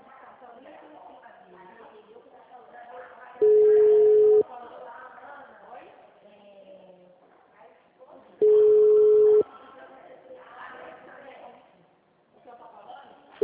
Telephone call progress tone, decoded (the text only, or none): ringback tone